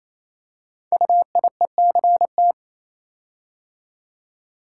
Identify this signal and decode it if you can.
Morse code, heard as a keyed tone, about 27 wpm, 700 Hz. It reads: UIECT
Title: Code Groups - 5 Letters - 28wpm